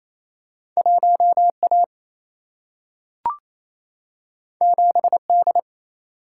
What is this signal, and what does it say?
Morse code, heard as a keyed tone, about 28 wpm, 700 Hz. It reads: 1 A 7 D
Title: Single-Letter Single-Number 28wpm